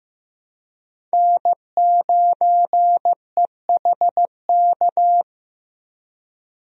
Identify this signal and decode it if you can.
Morse code, heard as a keyed tone, about 16 wpm, 700 Hz. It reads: N9EHK
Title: USA Call Signs - 5 Characters - Repeated 3x - 15wpm